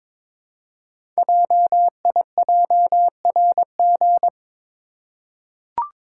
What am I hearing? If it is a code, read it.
Morse code, heard as a keyed tone, about 23 wpm, 700 Hz. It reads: JIJRG
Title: Code Groups - 5 Letters - 22wpm